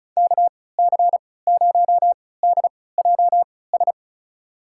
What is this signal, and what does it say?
Morse code, heard as a keyed tone, about 36 wpm, spaced as for 20 wpm, 700 Hz. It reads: KC0DJS
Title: USA Call Signs - 6 Characters - 3x Character Spacing - 35wpm